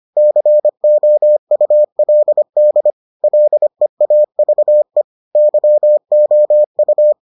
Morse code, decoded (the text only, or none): COULD LEAVE YOU